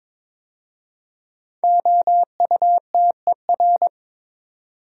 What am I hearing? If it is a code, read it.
Morse code, heard as a keyed tone, about 22 wpm, 700 Hz. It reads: OUTER
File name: 5 Letter Words - First 3 Letters Given - 22wpm